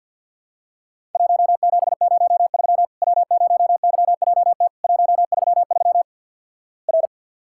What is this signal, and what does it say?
Morse code, heard as a keyed tone, about 50 wpm, 700 Hz. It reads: 1703 W0YJT 133 R
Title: Worked All Europe DX Contest - QTC Exchange - 50wpm